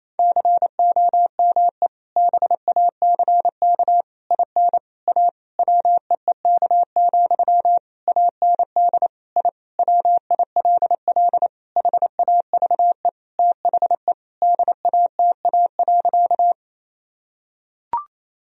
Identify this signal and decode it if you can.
Morse code, heard as a keyed tone, about 28 wpm, 700 Hz. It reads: COME BACK IN A WEEK, AND I WILL HAVE THE DATA.
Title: Sentences from Top 1000 Words - Review Part 3 - 28wpm